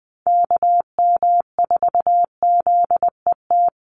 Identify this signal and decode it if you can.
Morse code, heard as a keyed tone, about 20 wpm, 700 Hz. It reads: KM4ZET
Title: USA Call Signs 20wpm